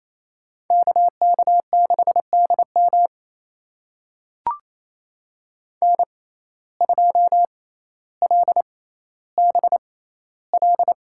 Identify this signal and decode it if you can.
Morse code, heard as a keyed tone, about 28 wpm, spaced as for 14 wpm, 700 Hz. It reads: KK6DM N2LBL
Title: USA Call Signs - 5 Characters - 6x Character Spacing - 28wpm